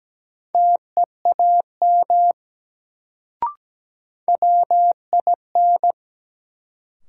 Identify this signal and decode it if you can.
Morse code, heard as a keyed tone, about 17 wpm, 700 Hz. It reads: TEAM WIN
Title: Sentences from Top 1500 Words - New Word Drill Alt - 17wpm